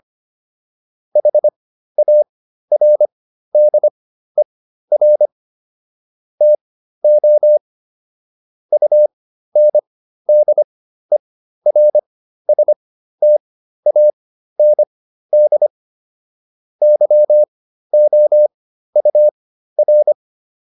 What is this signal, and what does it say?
Morse code, heard as a keyed tone, about 25 wpm, spaced as for 13 wpm, 600 Hz. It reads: HARDER TO UNDERSTAND YOUR